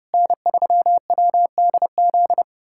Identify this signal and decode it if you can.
Morse code, heard as a keyed tone, about 30 wpm, 690 Hz. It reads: N3WDZ
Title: USA Callsign DE Callsign 30wpm